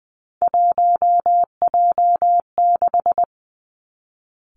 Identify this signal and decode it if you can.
Morse code, heard as a keyed tone, about 20 wpm, 700 Hz. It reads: 1J6
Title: Three Challenging Letter-Number Combinations 20wpm